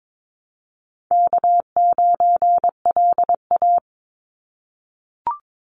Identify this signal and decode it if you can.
Morse code, heard as a keyed tone, about 22 wpm, 700 Hz. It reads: K9LA
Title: Call Signs - 22wpm